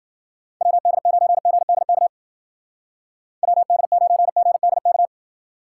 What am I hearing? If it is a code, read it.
Morse code, heard as a keyed tone, about 48 wpm, 700 Hz. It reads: WD9GDK WD9GDK
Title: Speed Racing - USA 2x3 Call Signs 40wpm